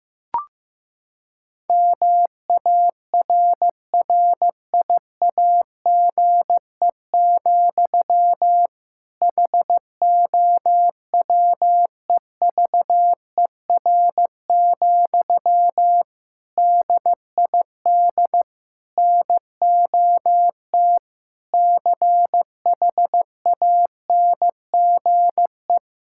Morse code, heard as a keyed tone, about 15 wpm, 700 Hz. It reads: MARRIAGE, HOWEVER, DID NOT CHANGE
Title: Sentences from Top 1000 Words - Review Part 5 - 15wpm